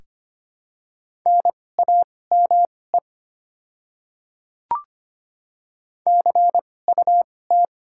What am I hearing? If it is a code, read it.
Morse code, heard as a keyed tone, about 25 wpm, spaced as for 18 wpm, 700 Hz. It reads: NAME CUT
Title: 2x Character Spacing - Top 300 Words - 25wpm